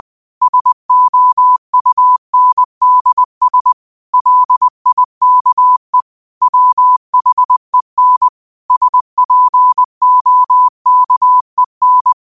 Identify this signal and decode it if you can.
Morse code, heard as a keyed tone, about 20 wpm, 1000 Hz. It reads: SOUNDS LIKE WHEN SPOKEN
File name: qotdfiles / 20wpm